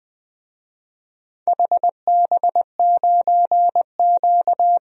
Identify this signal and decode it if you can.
Morse code, heard as a keyed tone, about 20 wpm, 700 Hz. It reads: HB9Q
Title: DXCC-100 Call Signs - 4 Characters - 20wpm